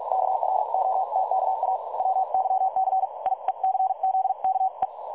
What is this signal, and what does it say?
Morse code, heard as a keyed tone, about 46 wpm, spaced as for 39 wpm, 760 Hz. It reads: KROTKA WYCIECZKE
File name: dlazaawansowanych / nagraniazpasma / websdr recording 3565.0kHz